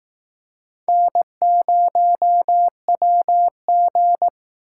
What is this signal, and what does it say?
Morse code, heard as a keyed tone, about 19 wpm, 700 Hz. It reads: N0WG
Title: Speed Racing - USA 1x2 Call Signs 15wpm